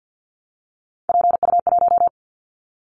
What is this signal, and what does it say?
Morse code, heard as a keyed tone, about 50 wpm, 700 Hz. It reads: PU1
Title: Callsign Prefixes for Top 50 DXCC - 50wpm